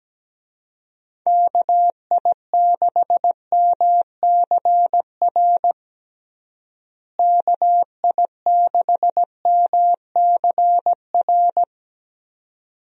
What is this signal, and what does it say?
Morse code, heard as a keyed tone, about 17 wpm, 700 Hz. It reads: KI6MCR KI6MCR
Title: USA Call Signs - 6 Characters - Repeated 2x - 17wpm